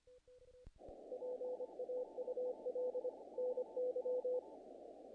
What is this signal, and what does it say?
Morse code, heard as a keyed tone, about 24 wpm, 500 Hz. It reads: PAUL NY
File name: contests / naqp-cw-k5tr-op-k5pi- -1800